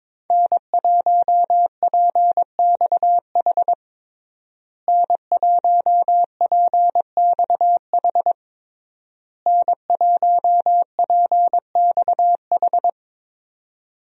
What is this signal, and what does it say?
Morse code, heard as a keyed tone, about 21 wpm, 700 Hz. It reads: N1PXH N1PXH N1PXH
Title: USA Call Signs - 5 Characters - Repeated 3x - 22wpm